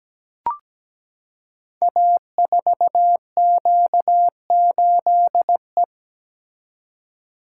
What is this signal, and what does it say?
Morse code, heard as a keyed tone, about 17 wpm, 700 Hz. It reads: A4Q8E
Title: Code Groups - 5 Letter:Numbers - 17wpm